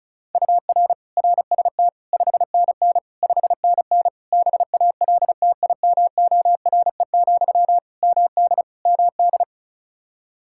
Morse code, heard as a keyed tone, about 34 wpm, 700 Hz. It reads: UR RST 5NN 5NN BALTIMORE, MD MD
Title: Practice QSO - Exchange 1 - Beginning - 35wpm